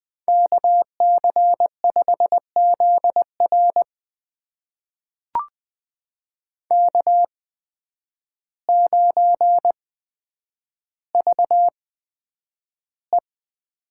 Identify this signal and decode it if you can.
Morse code, heard as a keyed tone, about 20 wpm, spaced as for 9 wpm, 700 Hz. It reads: KC5ZR K9VE
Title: USA Call Signs - 5 Characters - 8x Character Spacing - 20wpm